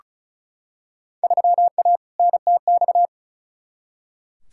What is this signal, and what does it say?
Morse code, heard as a keyed tone, about 34 wpm, 700 Hz. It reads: 3A NTX
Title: Field Day Exchanges - 35wpm